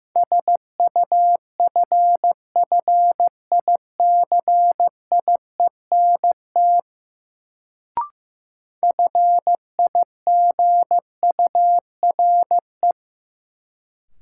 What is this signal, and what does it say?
Morse code, heard as a keyed tone, about 15 wpm, 700 Hz. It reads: SUFFICIENT FIGURE
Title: Sentences from Top 1400 Words - New Word Drill Alt - 15wpm